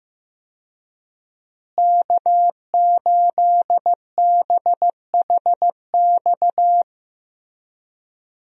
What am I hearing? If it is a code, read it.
Morse code, heard as a keyed tone, about 15 wpm, 700 Hz. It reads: K8BHX
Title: USA Call Signs - 5 Characters - Repeated 3x - 15wpm